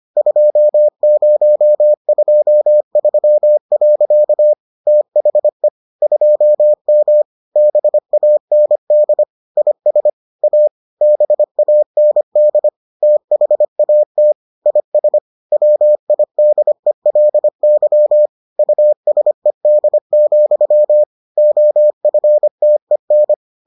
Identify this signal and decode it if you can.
Morse code, heard as a keyed tone, about 25 wpm, 600 Hz. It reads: 2023. THE 2M BAND IS A BAND THAT IS WIDELY USED, OFTEN